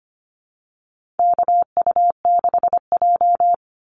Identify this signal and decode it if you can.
Morse code, heard as a keyed tone, about 25 wpm, 700 Hz. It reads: KU6J
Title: NA SOTA Chaser Callsigns 25wpm